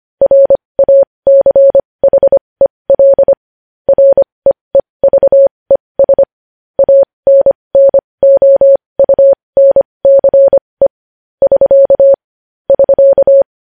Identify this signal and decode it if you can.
Morse code, heard as a keyed tone, about 25 wpm, spaced as for 20 wpm, 550 Hz. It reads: RACHEL REEVES ANNOUNCE <SK> <SK>
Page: news-headlines-in-morse-code-at-20-wpm